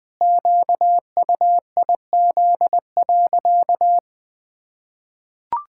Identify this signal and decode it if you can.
Morse code, heard as a keyed tone, about 21 wpm, 700 Hz. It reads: QUIZ.
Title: Pangram Challenge v2 - 20wpm